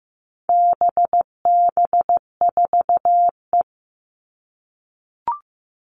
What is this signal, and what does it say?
Morse code, heard as a keyed tone, about 15 wpm, 700 Hz. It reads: BB4E
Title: All Cut Numbers - 4-Digit 15wpm